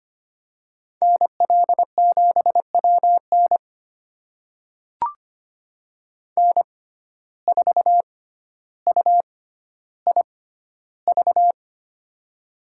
Morse code, heard as a keyed tone, about 25 wpm, spaced as for 13 wpm, 700 Hz. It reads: NL7WN N4UIV
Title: USA Call Signs - 5 Characters - 6x Character Spacing - 25wpm